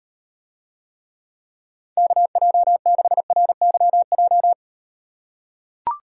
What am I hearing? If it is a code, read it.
Morse code, heard as a keyed tone, about 38 wpm, 700 Hz. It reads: KJ6RYJ